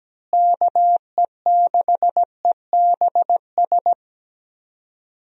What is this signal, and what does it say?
Morse code, heard as a keyed tone, about 17 wpm, 700 Hz. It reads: KE6EBS
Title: USA Call Signs - 6 Characters - Repeated 3x - 17wpm